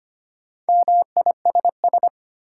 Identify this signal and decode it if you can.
Morse code, heard as a keyed tone, about 25 wpm, 700 Hz. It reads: MISS
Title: Sentences from Top 500 Words - New Word Drill - 25wpm